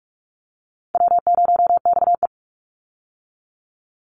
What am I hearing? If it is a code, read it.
Morse code, heard as a keyed tone, about 44 wpm, 700 Hz. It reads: W0XE
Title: Easy Call Signs - 45wpm